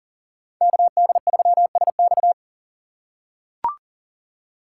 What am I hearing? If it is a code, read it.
Morse code, heard as a keyed tone, about 37 wpm, 700 Hz. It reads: KD3SX
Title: USA Call Signs - 5 Characters - 4x Character Spacing - 40wpm